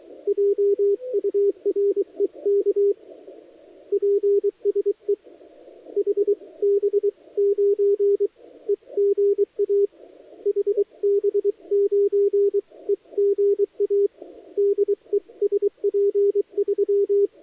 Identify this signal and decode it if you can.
Morse code, heard as a keyed tone, about 23 wpm, spaced as for 20 wpm, 400 Hz. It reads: JUREK PSE HB9EGA HB9EGA DE SP3
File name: dlazaawansowanych / nagraniazpasma / titawaNETcontrol